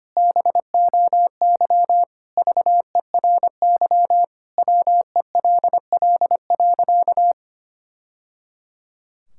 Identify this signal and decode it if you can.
Morse code, heard as a keyed tone, about 25 wpm, 700 Hz. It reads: BOY VERY WELL.
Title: Sentences from Top 1000 Words - Review Part 5 - 25wpm